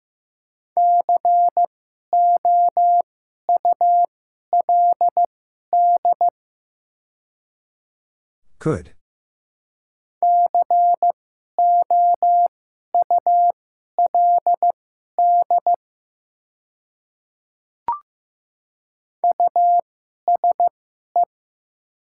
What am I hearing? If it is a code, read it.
Morse code, heard as a keyed tone, about 15 wpm, spaced as for 11 wpm, 700 Hz. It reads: COULD COULD USE
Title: 2x Character Spacing - Top 100 Words - 15wpm